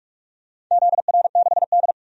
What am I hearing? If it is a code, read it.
Morse code, heard as a keyed tone, about 46 wpm, 700 Hz. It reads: ZR6D
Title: DXCC-100 Call Signs - 4 Characters - 45wpm